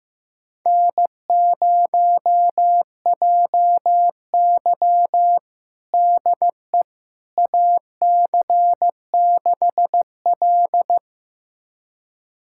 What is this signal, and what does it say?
Morse code, heard as a keyed tone, about 16 wpm, 700 Hz. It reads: N0JY DE AC6L
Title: USA Callsign DE Callsign - 4 Characters - 15wpm